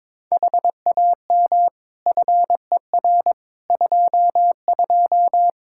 Morse code, heard as a keyed tone, about 22 wpm, 700 Hz. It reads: HAM FER 22
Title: Practice QSO - Exchange 3 - More Info - 22wpm